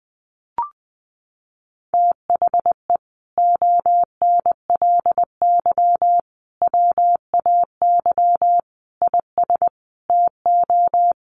Morse code, heard as a keyed tone, about 20 wpm, 700 Hz. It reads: THE ONLY WAY IS TO
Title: Sentences from Top 1500 Words 20wpm